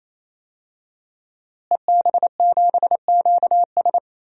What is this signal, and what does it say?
Morse code, heard as a keyed tone, about 28 wpm, 700 Hz. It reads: EB7QS
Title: DXCC-100 Call Signs - 5 Characters - Repeated 3x - 28wpm